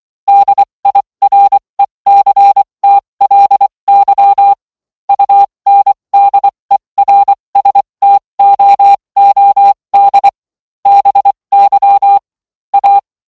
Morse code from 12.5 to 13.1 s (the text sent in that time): A